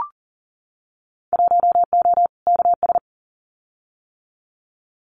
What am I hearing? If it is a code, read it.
Morse code, heard as a keyed tone, about 42 wpm, 700 Hz. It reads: 1O KS
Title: Winter Field Day Exchanges 40wpm